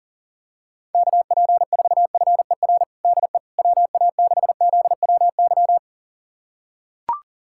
Morse code, heard as a keyed tone, about 37 wpm, 700 Hz. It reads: KP4FER DE WA6ZWY
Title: USA Callsign DE Callsign - 6 Characters - 40wpm